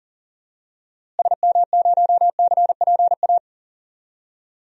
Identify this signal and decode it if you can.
Morse code, heard as a keyed tone, about 40 wpm, 700 Hz. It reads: SM0CPA
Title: DXCC-100 Call Signs - 6 Characters - Repeated 3x - 40wpm